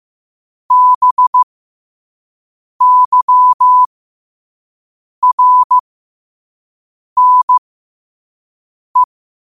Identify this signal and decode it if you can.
Morse code, heard as a keyed tone, about 15 wpm, spaced as for 5 wpm, 1000 Hz. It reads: BYRNE
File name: qotdfiles / 5wpm